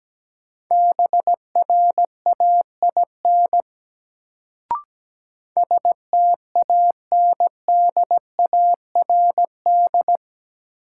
Morse code, heard as a keyed tone, about 17 wpm, 700 Hz. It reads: BRAIN STANDARD
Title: Sentences from Top 1000 Words - Review Part 6 - New Word Drill - 17wpm